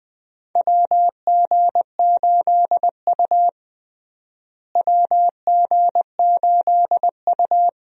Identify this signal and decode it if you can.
Morse code, heard as a keyed tone, about 20 wpm, 700 Hz. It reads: WG8U WG8U
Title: USA Call Signs - 4 Characters - Repeated 2x - 20wpm